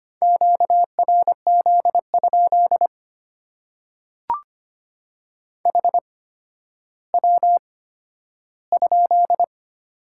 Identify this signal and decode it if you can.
Morse code, heard as a keyed tone, about 25 wpm, spaced as for 12 wpm, 700 Hz. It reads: QRZ? HW?
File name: Top 90 QSO Elements - 8x Character Spacing - 25wpm